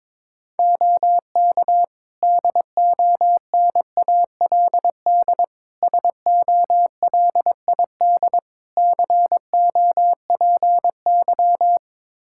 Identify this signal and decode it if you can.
Morse code, heard as a keyed tone, about 22 wpm, 700 Hz. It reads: OK DONALD SOLID COPY
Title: Practice QSO - Exchange 3 - More Info - 22wpm